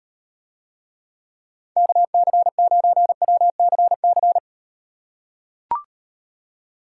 Morse code, heard as a keyed tone, about 38 wpm, 700 Hz. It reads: KC9WCC